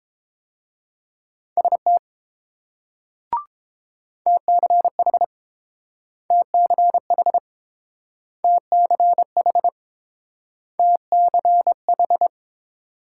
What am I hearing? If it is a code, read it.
Morse code, heard as a keyed tone, about 30 wpm, 700 Hz. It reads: ST TCH TCH TCH TCH